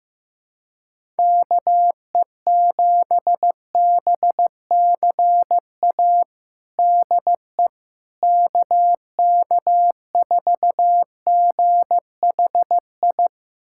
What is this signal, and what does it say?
Morse code, heard as a keyed tone, about 16 wpm, 700 Hz. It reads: KE7BCA DE KK4GHI
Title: USA Callsign DE Callsign 15wpm